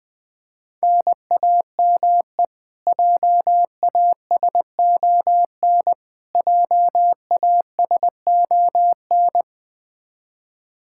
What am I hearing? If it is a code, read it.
Morse code, heard as a keyed tone, about 20 wpm, 700 Hz. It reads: NAME JASON JASON
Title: Practice QSO - Exchange 1 - Beginning - 20wpm